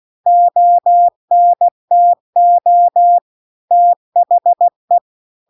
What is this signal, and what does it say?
Morse code, heard as a keyed tone, about 16 wpm, 700 Hz. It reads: ONTO THE